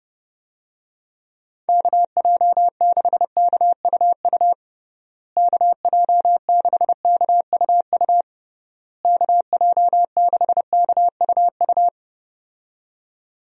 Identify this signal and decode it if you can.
Morse code, heard as a keyed tone, about 30 wpm, 700 Hz. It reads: KJ6KUU KJ6KUU KJ6KUU
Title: USA Call Signs - 6 Characters - Repeated 3x - 30wpm